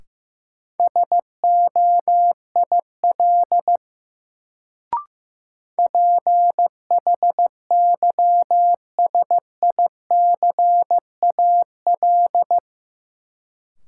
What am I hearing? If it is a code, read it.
Morse code, heard as a keyed tone, about 15 wpm, 700 Hz. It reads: SOIL PHYSICAL
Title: Sentences from Top 1000 Words - Review Part 4 - New Word Drill - 15wpm